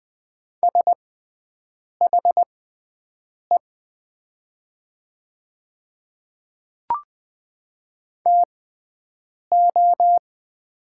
Morse code, heard as a keyed tone, about 20 wpm, spaced as for 7 wpm, 700 Hz. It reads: SHE TO
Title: 6x Character Spacing - Top 100 Words - 20wpm